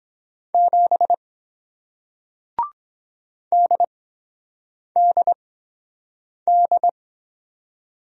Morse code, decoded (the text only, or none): 7 DDD